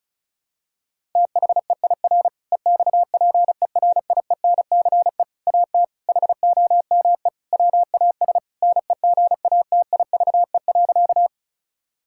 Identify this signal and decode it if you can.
Morse code, heard as a keyed tone, about 36 wpm, 700 Hz. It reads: THEIR EXPERIENCE AT HOME WAS NEGATIVE.
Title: Sentences from Top 1000 Words - Review Part 6 - 35wpm